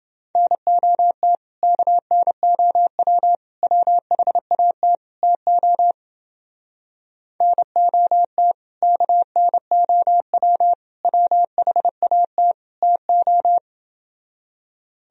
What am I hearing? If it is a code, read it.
Morse code, heard as a keyed tone, about 29 wpm, 700 Hz. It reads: NOT KNOW WHAT TO NOT KNOW WHAT TO